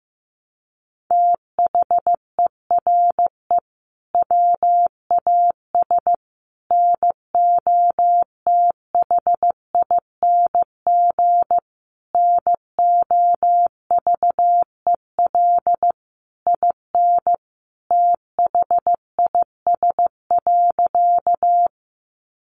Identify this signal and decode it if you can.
Morse code, heard as a keyed tone, about 15 wpm, 700 Hz. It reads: THERE WAS NOTHING NOVEL IN THIS.
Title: Sentences from Top 1300 Words 15wpm